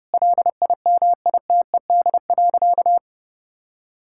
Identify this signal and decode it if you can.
Morse code, heard as a keyed tone, about 28 wpm, 700 Hz. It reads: LIMITED.
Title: Sentences from Top 1000 Words - Review Part 6 - 30wpm